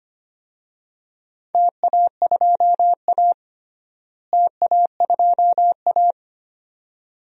Morse code, heard as a keyed tone, about 25 wpm, 700 Hz. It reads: TA2A TA2A
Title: DXCC-100 Call Signs - 4 Characters - Repeated 2x - 25wpm